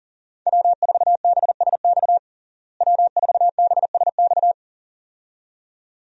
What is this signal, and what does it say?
Morse code, heard as a keyed tone, about 40 wpm, 700 Hz. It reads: W4BSX W4BSX
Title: USA Call Signs - 5 Characters - Repeated 2x - 40wpm